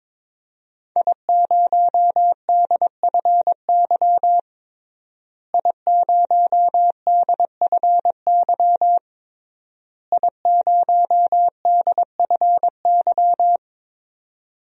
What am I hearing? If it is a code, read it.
Morse code, heard as a keyed tone, about 22 wpm, 700 Hz. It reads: I0DFY I0DFY I0DFY
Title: DXCC-100 Call Signs - 5 Characters - Repeated 3x - 22wpm